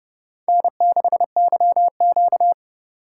Morse code, heard as a keyed tone, about 29 wpm, 700 Hz. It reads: N6YQ